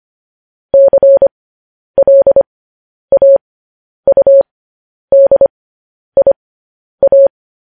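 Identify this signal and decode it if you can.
Morse code, heard as a keyed tone, about 25 wpm, spaced as for 10 wpm, 550 Hz. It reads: CLAUDIA
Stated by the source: Morse code news headlines practice transmission.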